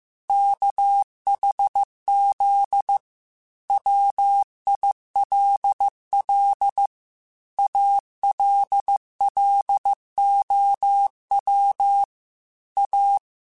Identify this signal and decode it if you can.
Morse code, heard as a keyed tone, about 15 wpm, 800 Hz. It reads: KHZ WILL ALLOW A
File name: Technical Subjects / Technical Subjects-mp3s / Sw Pwr Sup @13